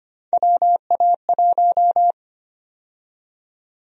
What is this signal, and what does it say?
Morse code, heard as a keyed tone, about 25 wpm, 700 Hz. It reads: WA1
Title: USA Call Sign Prefixes - Repeated 2x - 25wpm